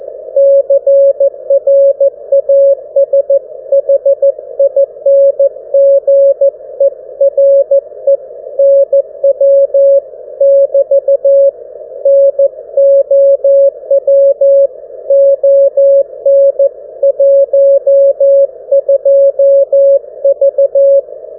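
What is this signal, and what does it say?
Morse code, heard as a keyed tone, about 14 wpm, 550 Hz. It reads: CRASHING ERE NW <BT> NOW ON 12V
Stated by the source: GW4HXO 1W QRP Mini-QTX with B-Hive VFO, 472 kHz MF QSO with GW3UEP.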